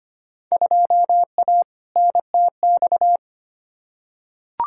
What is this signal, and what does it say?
Morse code, heard as a keyed tone, about 25 wpm, 700 Hz. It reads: 2A NTX
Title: Field Day Exchanges - 25wpm